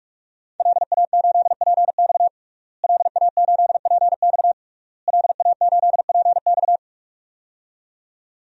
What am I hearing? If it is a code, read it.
Morse code, heard as a keyed tone, about 41 wpm, 700 Hz. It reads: LA8PX LA8PX LA8PX
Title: DXCC-100 Call Signs - 5 Characters - Repeated 3x - 45wpm